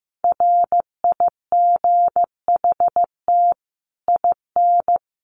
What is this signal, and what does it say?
Morse code, heard as a keyed tone, about 15 wpm, 700 Hz. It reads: RIGHT IN
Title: Sets of 5 Words 15wpm